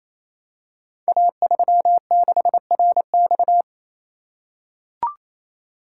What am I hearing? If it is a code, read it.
Morse code, heard as a keyed tone, about 28 wpm, 700 Hz. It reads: A36RX
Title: Code Groups - 5 Letter:Numbers - 28wpm